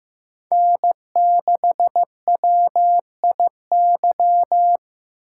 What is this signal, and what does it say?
Morse code, heard as a keyed tone, about 15 wpm, 700 Hz. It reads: N6WIY
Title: USA Call Signs - 5 Characters - 6x Character Spacing - 15wpm